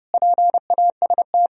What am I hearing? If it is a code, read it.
Morse code, heard as a keyed tone, about 30 wpm, 700 Hz. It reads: PAST
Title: Sentences from Top 1000 Words - Review Part 3 - 30wpm